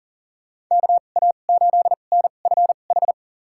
Morse code, heard as a keyed tone, about 40 wpm, spaced as for 29 wpm, 700 Hz. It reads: KA8NFH
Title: USA Call Signs - 6 Characters - 2x Character Spacing - 40wpm